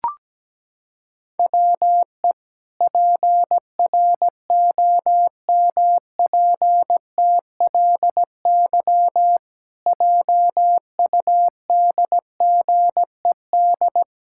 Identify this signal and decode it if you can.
Morse code, heard as a keyed tone, about 17 wpm, 700 Hz. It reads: WE PROMPTLY JUDGED
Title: Pangram Challenge v2 - 17wpm